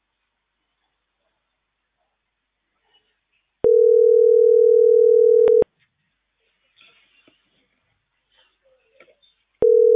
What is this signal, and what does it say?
Ringback tone (North America), 440 + 480 Hz.